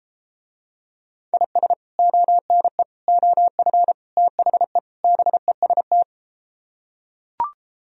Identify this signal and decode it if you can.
Morse code, heard as a keyed tone, about 32 wpm, 700 Hz. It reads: IS ONE OF THE BEST